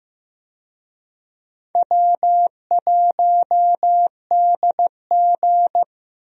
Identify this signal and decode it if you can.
Morse code, heard as a keyed tone, about 15 wpm, 700 Hz. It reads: W1DG
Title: USA Call Signs - 4 Characters - Repeated 3x - 15wpm